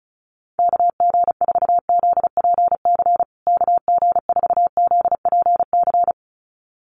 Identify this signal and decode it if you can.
Morse code, heard as a keyed tone, about 34 wpm, 700 Hz. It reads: KG4ZPC KG4ZPC
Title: USA Call Signs Repeated 2x 35wpm